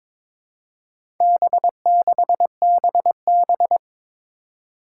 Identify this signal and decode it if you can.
Morse code, heard as a keyed tone, about 23 wpm, 700 Hz. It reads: B6BB
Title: Four Challenging Letter-Number Combinations 22wpm